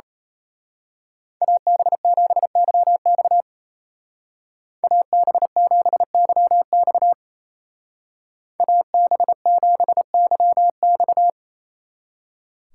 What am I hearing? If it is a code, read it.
Morse code, heard as a keyed tone, about 33 wpm, 700 Hz. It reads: AB7YX AB7YX AB7YX
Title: Speed Racing - USA 2x2 Call Signs 28wpm